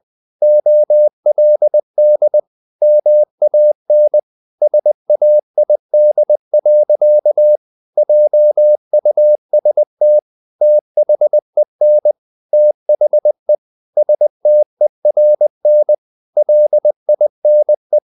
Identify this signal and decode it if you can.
Morse code, heard as a keyed tone, about 20 wpm, 600 Hz. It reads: OLD MAN SAID. JUST THEN THE STERN LINE
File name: the old man and the sea / the old man and the sea part 1 20wpm